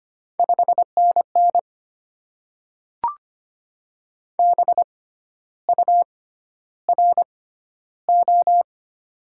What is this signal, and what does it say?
Morse code, heard as a keyed tone, about 25 wpm, spaced as for 12 wpm, 690 Hz. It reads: 5NN BURO
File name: Top 90 QSO Elements - 6x Character Spacing - 25wpm